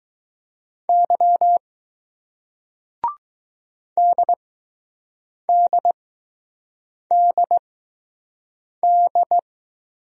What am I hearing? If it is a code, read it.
Morse code, heard as a keyed tone, about 20 wpm, spaced as for 6 wpm, 700 Hz. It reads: Y DDDD